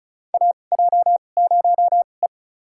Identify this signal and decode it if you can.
Morse code, heard as a keyed tone, about 34 wpm, spaced as for 25 wpm, 700 Hz. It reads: AJ0E
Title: USA Call Signs - 2x Character Spacing - 35wpm